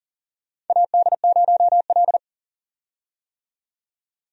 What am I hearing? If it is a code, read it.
Morse code, heard as a keyed tone, about 40 wpm, 700 Hz. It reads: AD0L
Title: USA Call Signs - 4 Characters - 40wpm